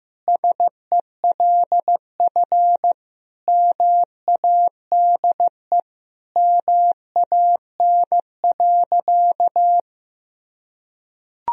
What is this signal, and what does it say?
Morse code, heard as a keyed tone, about 16 wpm, 700 Hz. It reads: SELF MADE MAN.
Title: Sentences from Top 1000 Words - Review Part 1 - 15wpm